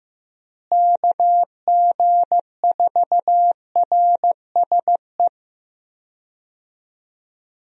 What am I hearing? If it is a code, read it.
Morse code, heard as a keyed tone, about 15 wpm, 700 Hz. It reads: KG4RSE
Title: USA Call Signs - 6 Characters - Repeated 2x - 15wpm